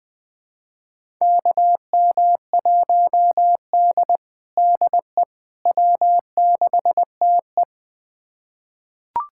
Morse code, heard as a keyed tone, about 20 wpm, 700 Hz. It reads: KM1D DE W6TE
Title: USA Callsign DE Callsign - 4 Characters - 20wpm